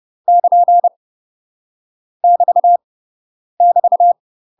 Morse code, heard as a keyed tone, about 30 wpm, 700 Hz. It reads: <KN> <BT> <BT>
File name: UK MASTER QSOs TRAINING / 30WPM / UK MASTER QSOs 0008